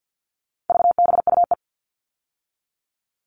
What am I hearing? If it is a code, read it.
Morse code, heard as a keyed tone, about 50 wpm, 700 Hz. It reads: VBUE
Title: All Cut Numbers - 4-Digit 50wpm